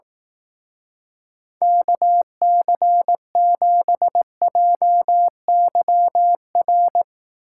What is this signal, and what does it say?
Morse code, heard as a keyed tone, about 18 wpm, 700 Hz. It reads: KC7JYR
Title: Speed Racing - USA 2x3 Call Signs 15wpm